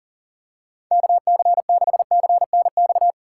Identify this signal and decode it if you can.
Morse code, heard as a keyed tone, about 40 wpm, 700 Hz. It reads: KC6CNX
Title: USA Call Signs - 6 Characters - 2x Character Spacing - 40wpm